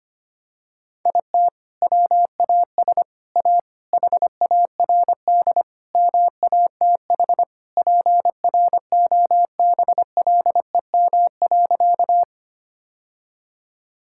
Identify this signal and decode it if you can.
Morse code, heard as a keyed tone, about 25 wpm, 700 Hz. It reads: IT WAS A HARD MATH PROBLEM.
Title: Sentences from Top 1000 Words - Review Part 6 - 25wpm